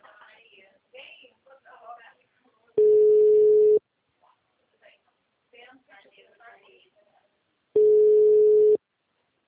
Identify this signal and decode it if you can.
Ringback tone (Europe, 425 Hz), with speech.